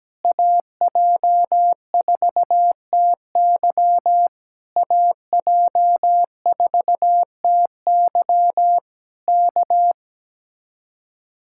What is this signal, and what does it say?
Morse code, heard as a keyed tone, about 17 wpm, 700 Hz. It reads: AJ4TY AJ4TY K
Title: Speed-Racing Common QSO Phrases - 17wpm